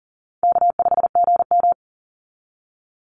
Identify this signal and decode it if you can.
Morse code, heard as a keyed tone, about 38 wpm, 700 Hz. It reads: K5GM
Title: Call Signs - 40wpm